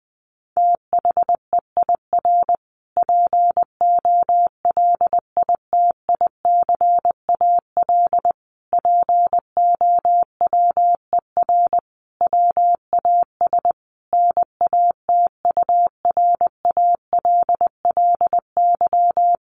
Morse code, heard as a keyed tone, about 20 wpm, 700 Hz. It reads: THEIR POLITICAL POWER WAS NATURALLY
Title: Sentences from Top 1300 Words 20wpm